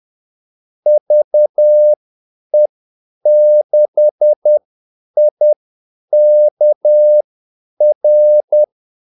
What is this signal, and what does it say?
Morse code, heard as a keyed tone, about 10 wpm, spaced as for 8 wpm, 600 Hz. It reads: VE6IKR
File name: Callsigns-10-8wpm